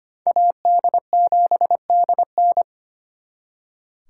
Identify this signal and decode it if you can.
Morse code, heard as a keyed tone, about 32 wpm, 700 Hz. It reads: AD7DN